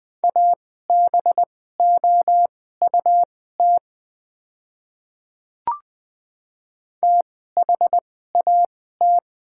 Morse code, heard as a keyed tone, about 21 wpm, spaced as for 15 wpm, 700 Hz. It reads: ABOUT THAT
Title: 2x Character Spacing - Top 100 Words - 20wpm